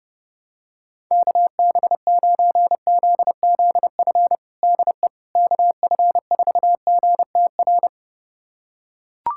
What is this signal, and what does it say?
Morse code, heard as a keyed tone, about 32 wpm, 700 Hz. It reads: KB9ZZF DE KF4GTR
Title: USA Callsign DE Callsign - 6 Characters - 30wpm